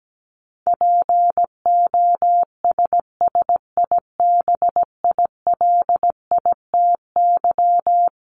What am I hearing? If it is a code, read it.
Morse code, heard as a keyed tone, about 17 wpm, 700 Hz. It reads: POSSIBILITY
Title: Top 1751-2000 Words 17wpm